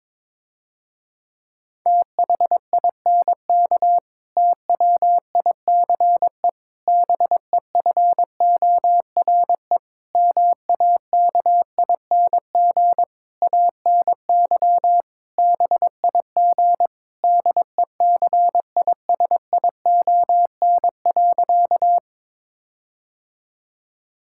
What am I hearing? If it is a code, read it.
Morse code, heard as a keyed tone, about 22 wpm, 700 Hz. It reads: THINK TWICE BEFORE MAKING ANY BIG DECISION.
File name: Difficult Sentences with Hints - 22wpm